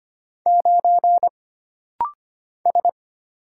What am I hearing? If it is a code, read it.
Morse code, heard as a keyed tone, about 25 wpm, 700 Hz. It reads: 9 S
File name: Single Character (Letter:Number:Punctuation:Prosign) - Rapid-Fire - 25wpm